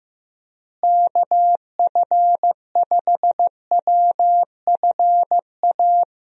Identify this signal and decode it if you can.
Morse code, heard as a keyed tone, about 16 wpm, 700 Hz. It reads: KF5WFA
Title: USA Call Signs - 6 Characters - Repeated 2x - 15wpm